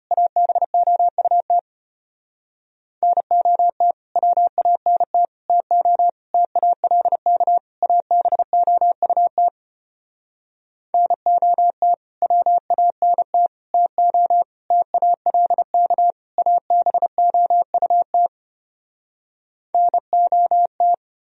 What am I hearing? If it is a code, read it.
Morse code, heard as a keyed tone, about 33 wpm, 700 Hz. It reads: ABOUT NOT WANT TO TALK ABOUT NOT WANT TO TALK ABOUT NOT